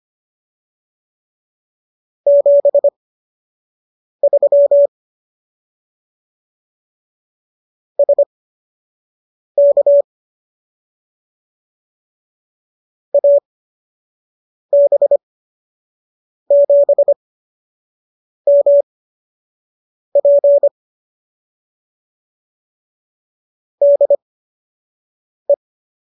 Morse code, heard as a keyed tone, about 25 wpm, spaced as for 6 wpm, 570 Hz. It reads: 73 SK AB7MP DE